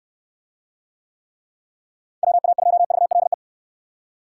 Morse code, heard as a keyed tone, about 67 wpm, 700 Hz. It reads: WA2VLE